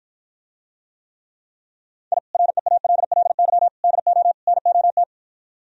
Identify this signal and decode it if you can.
Morse code, heard as a keyed tone, about 52 wpm, 700 Hz. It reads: I REALLY DO NOT